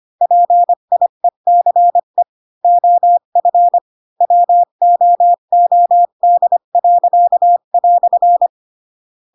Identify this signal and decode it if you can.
Morse code, heard as a keyed tone, about 25 wpm, 700 Hz. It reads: PIECE OF WOOD."